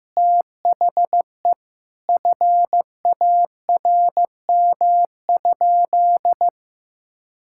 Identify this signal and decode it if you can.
Morse code, heard as a keyed tone, about 15 wpm, 700 Hz. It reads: THE FARM?
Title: Sentences from Top 1000 Words - Review Part 6 - 15wpm